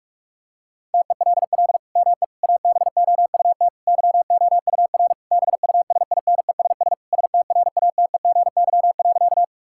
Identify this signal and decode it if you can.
Morse code, heard as a keyed tone, about 46 wpm, 700 Hz. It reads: TELL ME ABOUT YOUR BUSINESS STRATEGY.
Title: Sentences from Top 1000 Words - Review Part 5 - 45wpm